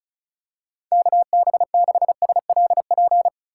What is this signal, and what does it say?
Morse code, heard as a keyed tone, about 36 wpm, 700 Hz. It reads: KB6SLP
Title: USA Call Signs - 6 Characters - 4x Character Spacing - 35wpm